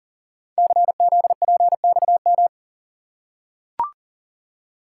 Morse code, heard as a keyed tone, about 38 wpm, 700 Hz. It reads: CZPXM
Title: Code Groups - 5 Letters - 40wpm